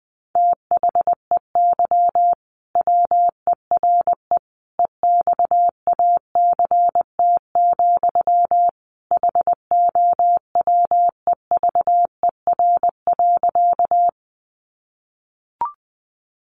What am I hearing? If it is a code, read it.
Morse code, heard as a keyed tone, about 20 wpm, 700 Hz. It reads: THEY WERE EXACT, HOWEVER.
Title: Sentences from Top 1300 Words 20wpm